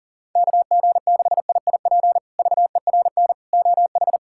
Morse code, heard as a keyed tone, about 37 wpm, 700 Hz. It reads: KG6IIP VERN OH
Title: K1USN SST QSO Practice - 40wpm